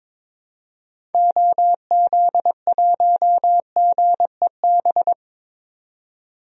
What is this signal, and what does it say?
Morse code, heard as a keyed tone, about 22 wpm, 700 Hz. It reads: OZ1GEB
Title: DXCC-100 Call Signs - 6 Characters - 22wpm